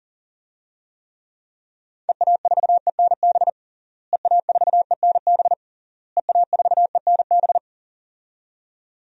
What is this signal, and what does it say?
Morse code, heard as a keyed tone, about 40 wpm, 700 Hz. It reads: EA4ENB EA4ENB EA4ENB
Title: DXCC-100 Call Signs - 6 Characters - Repeated 3x - 40wpm